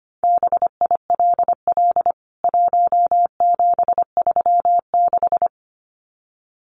Morse code, heard as a keyed tone, about 25 wpm, 700 Hz. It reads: BILL 1736
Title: CWT Exchange 25wpm